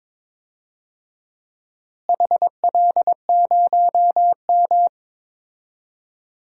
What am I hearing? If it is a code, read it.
Morse code, heard as a keyed tone, about 22 wpm, 700 Hz. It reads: HL0M
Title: DXCC-100 Call Signs - 4 Characters - Repeated 3x - 22wpm